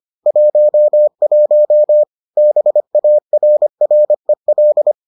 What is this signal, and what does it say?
Morse code, heard as a keyed tone, about 25 wpm, 600 Hz. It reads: 11 BARREL